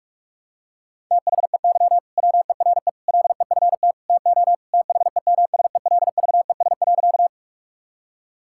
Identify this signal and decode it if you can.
Morse code, heard as a keyed tone, about 41 wpm, 700 Hz. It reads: THEY WERE LEFT TO THEMSELVES.
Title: Sentences from Top 1000 Words - Review Part 5 - 45wpm